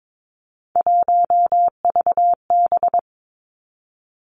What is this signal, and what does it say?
Morse code, heard as a keyed tone, about 22 wpm, 700 Hz. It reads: 1VB
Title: Three Challenging Letter-Number Combinations 22wpm